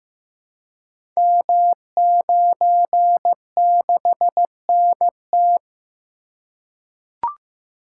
Morse code, heard as a keyed tone, about 15 wpm, 700 Hz. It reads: M96NT
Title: Code Groups - 5 Letter:Numbers - 15wpm